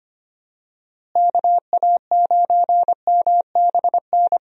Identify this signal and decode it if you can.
Morse code, heard as a keyed tone, about 25 wpm, 700 Hz. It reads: KA9MBN
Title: USA Call Signs - 6 Characters - Repeated 2x - 25wpm